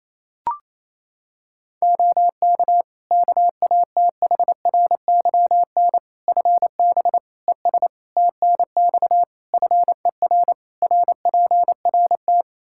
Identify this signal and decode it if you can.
Morse code, heard as a keyed tone, about 28 wpm, 700 Hz. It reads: OK KATHRYN FB ES TNX FER RPRT
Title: K1USN SST QSO Practice 28wpm